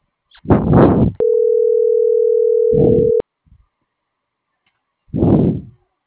Ringback tone (North America), 440 + 480 Hz.